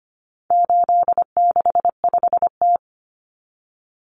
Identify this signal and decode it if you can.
Morse code, heard as a keyed tone, about 25 wpm, 700 Hz. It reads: 865T
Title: Common Cut Numbers - 4-Digit 25wpm